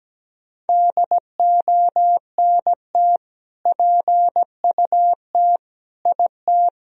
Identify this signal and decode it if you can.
Morse code, heard as a keyed tone, about 17 wpm, 700 Hz. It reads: DONT PUT IT
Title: Sentences from Top 1000 Words - Review Part 2 - 17wpm